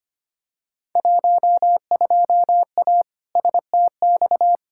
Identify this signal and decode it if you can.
Morse code, heard as a keyed tone, about 25 wpm, 700 Hz. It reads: 12A STX
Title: Field Day Exchanges Encore - 25wpm